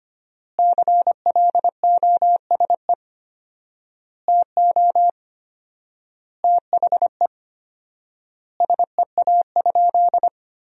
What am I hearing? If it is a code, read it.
Morse code, heard as a keyed tone, about 25 wpm, 700 Hz. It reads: CLOSE TO THE SEA?
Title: 4x Word-Spacing - Sentences from Top 300 Words - 25wpm